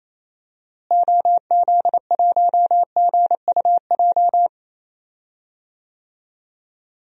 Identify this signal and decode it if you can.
Morse code, heard as a keyed tone, about 28 wpm, 700 Hz. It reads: OZ1GUJ
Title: DXCC-100 Call Signs - 6 Characters - Repeated 3x - 28wpm